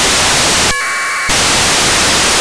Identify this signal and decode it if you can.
Packet radio (AX.25, as in APRS).
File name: uploader / 14.02.2016 psat